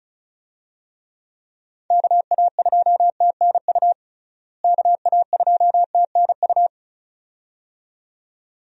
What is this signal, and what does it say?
Morse code, heard as a keyed tone, about 36 wpm, 700 Hz. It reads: KA2TNU KA2TNU
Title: USA Call Signs - 6 Characters - Repeated 2x - 35wpm